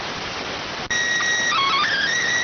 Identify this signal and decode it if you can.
SSTV (slow-scan television), PD 290.